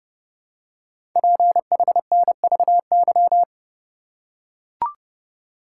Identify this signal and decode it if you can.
Morse code, heard as a keyed tone, about 30 wpm, 700 Hz. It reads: PHNVY
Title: Code Groups - 5 Letters - 30wpm